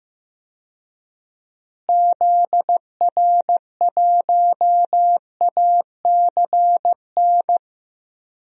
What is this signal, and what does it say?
Morse code, heard as a keyed tone, about 15 wpm, 700 Hz. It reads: ZR1ACN
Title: DXCC-100 Call Signs - 6 Characters - Repeated 3x - 15wpm